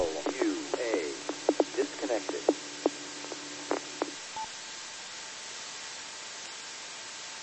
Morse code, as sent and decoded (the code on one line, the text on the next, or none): none